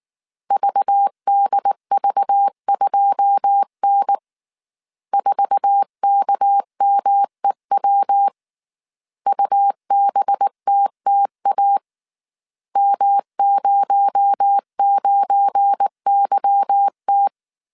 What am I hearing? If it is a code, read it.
Morse code, heard as a keyed tone, about 19 wpm, 800 Hz. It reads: VDV2N 4KMEW UBTTA M09YT
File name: CW / DET / 1 9 1500C S A 18wpm 2459